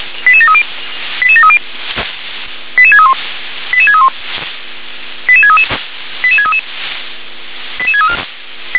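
Five-tone selective call (ZVEI), heard as quick runs of five tones, tones of about 70 ms. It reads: 88730 88730 88631 88631 88633 88633 88634
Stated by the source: ZVEI five-tone call 88085 received on an SDR.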